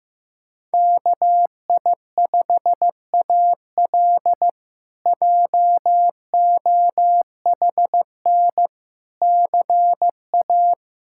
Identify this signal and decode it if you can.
Morse code, heard as a keyed tone, about 15 wpm, 700 Hz. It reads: KI5AL JOHN CA
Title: K1USN SST QSO Practice - 15wpm